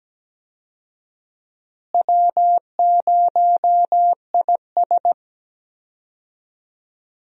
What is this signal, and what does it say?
Morse code, heard as a keyed tone, about 17 wpm, 700 Hz. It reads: W0IS
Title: USA Call Signs - 4 Characters - Repeated 3x - 17wpm